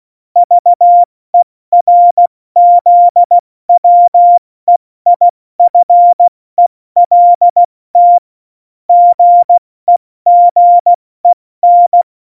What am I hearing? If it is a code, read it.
Morse code, heard as a keyed tone, about 16 wpm, 700 Hz. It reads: VERZWEIFELT GEGEN
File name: Morsekurs / Witze / Witz 25. September 2018, 70 F80